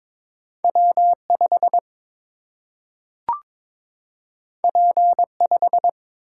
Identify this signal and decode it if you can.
Morse code, heard as a keyed tone, about 22 wpm, 700 Hz. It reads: W 5 P 5
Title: Single-Letter Single-Number 22wpm